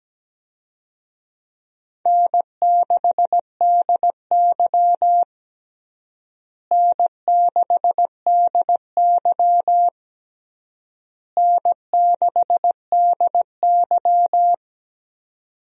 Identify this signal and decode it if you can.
Morse code, heard as a keyed tone, about 17 wpm, 700 Hz. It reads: N6DY N6DY N6DY
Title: USA Call Signs - 4 Characters - Repeated 3x - 17wpm